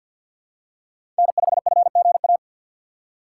Morse code, heard as a keyed tone, about 53 wpm, 700 Hz. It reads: N5FGA
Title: USA Call Signs - 5 Characters - 4x Character Spacing - 50wpm